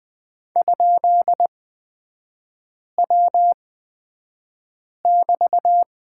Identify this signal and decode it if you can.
Morse code, heard as a keyed tone, about 20 wpm, spaced as for 5 wpm, 700 Hz. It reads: ?W<BT>
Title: 250 Lesson 40e - BK - Three Characters 20wpm